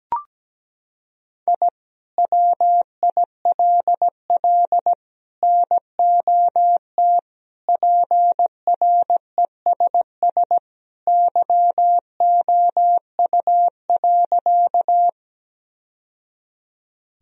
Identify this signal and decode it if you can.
Morse code, heard as a keyed tone, about 17 wpm, 700 Hz. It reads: I WILL NOT PRESS YOU.
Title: Sentences from Top 1000 Words - Review Part 1 - 17wpm